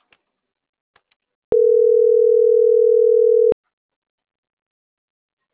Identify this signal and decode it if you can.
Ringback tone (North America), 440 + 480 Hz.